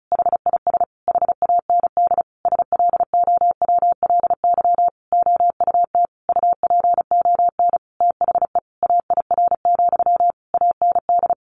Morse code, heard as a keyed tone, about 34 wpm, 700 Hz. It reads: HIS HAND SLOWLY OUT UPON THE AIR, AND
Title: A Sound Of Thunder 35wpm